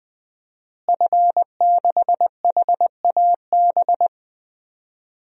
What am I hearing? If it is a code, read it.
Morse code, heard as a keyed tone, about 20 wpm, 700 Hz. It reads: F6HAB
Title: DXCC-100 Call Signs - 5 Characters - Repeated 3x - 20wpm